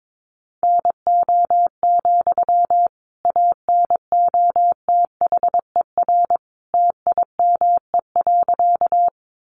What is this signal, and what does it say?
Morse code, heard as a keyed tone, about 22 wpm, 700 Hz. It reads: NO, ANOTHER TIME.
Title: Sentences from Top 600 Words 22wpm